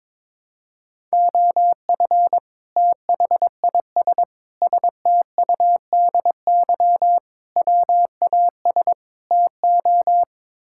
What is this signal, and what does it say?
Morse code, heard as a keyed tone, about 22 wpm, 700 Hz. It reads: OF THIS STUDY WAS TO
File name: practice / 22wpm / Build A Phrase - 5-Word Phrases - 22wpm